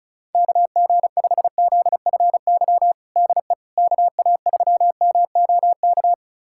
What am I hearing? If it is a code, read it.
Morse code, heard as a keyed tone, about 36 wpm, 700 Hz. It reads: KG5ZFY DE KA3MOK
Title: USA Callsign DE Callsign - 6 Characters - 35wpm